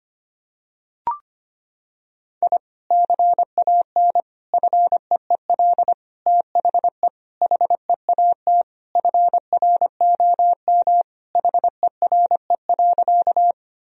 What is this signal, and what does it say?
Morse code, heard as a keyed tone, about 25 wpm, 700 Hz. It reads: I CAN FEEL THE HEAT FROM HERE.
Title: Sentences from Top 1000 Words - Review Part 3 - 25wpm